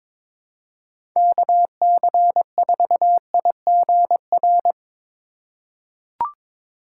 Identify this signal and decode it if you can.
Morse code, heard as a keyed tone, about 22 wpm, 700 Hz. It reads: KC4IGR
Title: USA Call Signs - 6 Characters - 3x Character Spacing - 22wpm